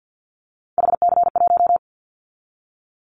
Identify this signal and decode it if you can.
Morse code, heard as a keyed tone, about 50 wpm, 700 Hz. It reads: HC1
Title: Callsign Prefixes for Top 100 DXCC - 50wpm